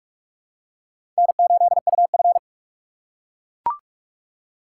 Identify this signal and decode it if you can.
Morse code, heard as a keyed tone, about 43 wpm, 700 Hz. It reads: N8UF
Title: USA Call Signs - 4 Characters - 8x Character Spacing - 45wpm